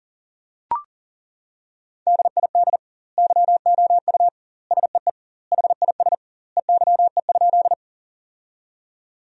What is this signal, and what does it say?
Morse code, heard as a keyed tone, about 40 wpm, 700 Hz. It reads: DID YOU SEE HIS EYE?
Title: 2x Word-Spacing - Sentences from Top 300 Words - 40wpm